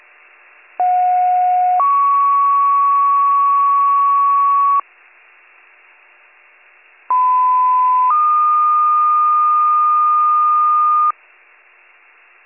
Two-tone page, heard as a one-second tone, then a three-second tone, about 730 then 1120 Hz, about 1010 then 1220 Hz. It is heard from 0.8 to 11.1 s.